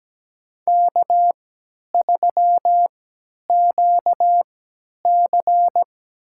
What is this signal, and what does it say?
Morse code, heard as a keyed tone, about 17 wpm, spaced as for 10 wpm, 700 Hz. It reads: K3QC
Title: USA Call Signs - 4 Characters - 3x Character Spacing - 17wpm